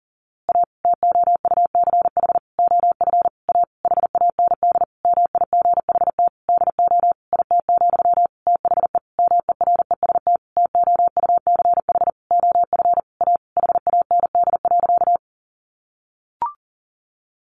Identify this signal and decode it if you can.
Morse code, heard as a keyed tone, about 40 wpm, 700 Hz. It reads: A TOUCH OF A HAND MIGHT DO IT, THE MEREST TOUCH OF A HAND.
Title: A Sound Of Thunder 40wpm